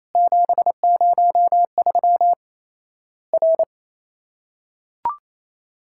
Morse code, heard as a keyed tone, about 28 wpm, 700 Hz. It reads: 703 R
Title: Worked All Europe DX Contest - QTC Exchange - 28wpm